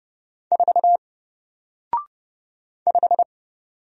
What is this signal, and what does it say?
Morse code, heard as a keyed tone, about 29 wpm, 700 Hz. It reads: 4 5
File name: Single-Letter-Number-30wpm